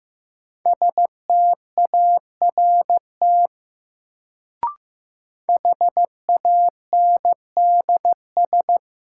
Morse code, heard as a keyed tone, about 15 wpm, 700 Hz. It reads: START HANDS
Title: Sets of 4 Words - New Word Drill - 15wpm